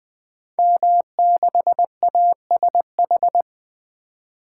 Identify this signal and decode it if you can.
Morse code, heard as a keyed tone, about 20 wpm, 700 Hz. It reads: M6ASH
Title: Code Groups - 5 Letter:Numbers - 20wpm